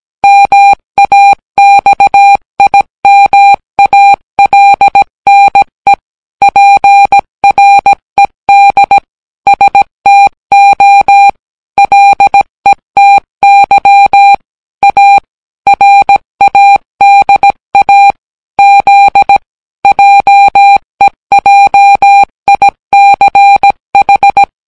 Morse code, heard as a keyed tone, about 17 wpm, 800 Hz. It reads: MAXIMALNE PRED STO LETY A RADA Z JEJICH